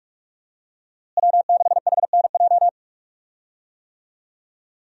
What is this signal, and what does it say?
Morse code, heard as a keyed tone, about 46 wpm, 700 Hz. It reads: W6HNJ
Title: USA Call Signs - 5 Characters - Repeated 3x - 45wpm